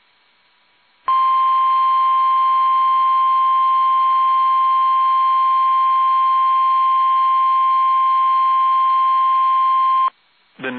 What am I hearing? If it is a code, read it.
Emergency Alert System (SAME) tones (the 1050 Hz attention tone). It is heard from 1.0 to 10.1 s.
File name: eas / KUSO / KUSO